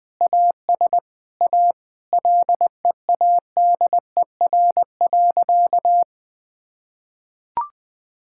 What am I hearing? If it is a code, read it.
Morse code, heard as a keyed tone, about 20 wpm, 700 Hz. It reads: AS A LEADER.
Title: Sentences with 2-Word Phrases - Spoken Beginning - 20wpm